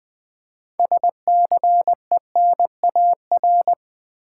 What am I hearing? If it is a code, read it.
Morse code, heard as a keyed tone, about 20 wpm, 690 Hz. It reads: SCENAR
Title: Build A Word - 2 Letters At A Time - 8-Letter Words - 20wpm